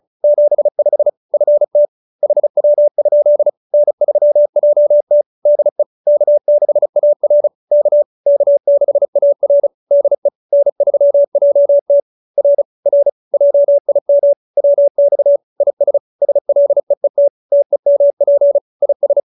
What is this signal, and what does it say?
Morse code, heard as a keyed tone, about 34 wpm, 570 Hz. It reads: 75 FT HW? N3JT DE K6AR K K6AR DE N3JT R R JIM WX IS SLEET TEMP IS